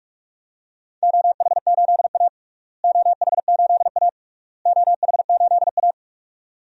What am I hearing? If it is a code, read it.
Morse code, heard as a keyed tone, about 46 wpm, 700 Hz. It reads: OH8A OH8A OH8A
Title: DXCC-100 Call Signs - 4 Characters - Repeated 3x - 45wpm